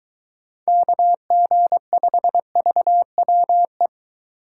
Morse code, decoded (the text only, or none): KG5VWE